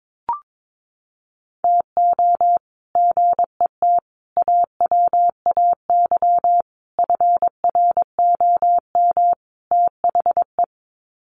Morse code, heard as a keyed tone, about 22 wpm, 700 Hz. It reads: TO GET AWAY FROM THE
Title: Sets of 5 Words Encore2 22wpm